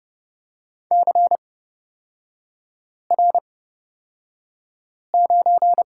Morse code, heard as a keyed tone, about 30 wpm, spaced as for 5 wpm, 700 Hz. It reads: CR9
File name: Lesson 37e - Letter x - Three Characters 30wpm